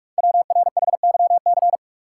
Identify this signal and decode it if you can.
Morse code, heard as a keyed tone, about 44 wpm, 690 Hz. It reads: WRHYC
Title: Code Groups - 5 Letters - 45wpm